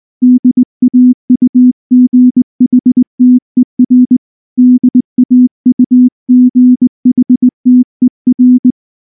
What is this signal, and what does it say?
Morse code, heard as a keyed tone, about 20 wpm, 250 Hz. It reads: DAUGHTER DAUGHTER